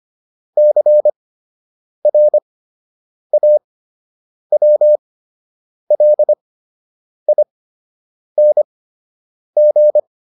Morse code, heard as a keyed tone, about 25 wpm, spaced as for 8 wpm, 600 Hz. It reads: CRAWLING
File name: ING8 08